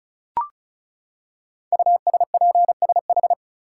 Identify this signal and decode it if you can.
Morse code, heard as a keyed tone, about 36 wpm, 700 Hz. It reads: USPSH
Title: Code Groups - 5 Letters - 35wpm